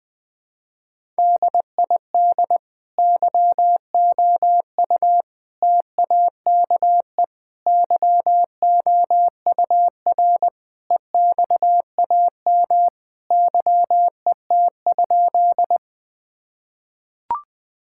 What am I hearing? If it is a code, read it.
Morse code, heard as a keyed tone, about 20 wpm, 700 Hz. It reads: DID YOU TAKE YOUR EXAM YET?
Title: Sentences from Top 1000 Words - Review Part 5 - 20wpm